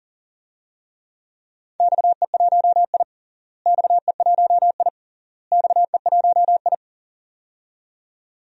Morse code, heard as a keyed tone, about 37 wpm, 700 Hz. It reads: XE1I XE1I XE1I
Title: DXCC-100 Call Signs - 4 Characters - Repeated 3x - 40wpm